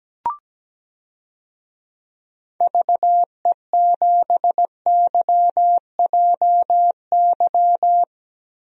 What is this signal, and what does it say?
Morse code, heard as a keyed tone, about 17 wpm, 700 Hz. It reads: VE7YJY
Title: DXCC-100 Call Signs - 6 Characters - Repeated 2x - 17wpm